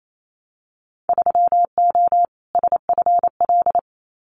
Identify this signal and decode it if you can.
Morse code, heard as a keyed tone, about 28 wpm, 700 Hz. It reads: 3O SFL
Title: Winter Field Day Exchanges 28wpm